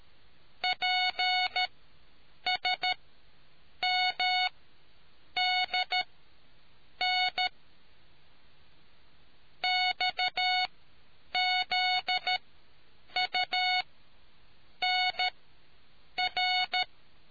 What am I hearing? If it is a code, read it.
Morse code, heard as a keyed tone, about 13 wpm, spaced as for 7 wpm, 730 Hz. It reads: PSMDN XZUNR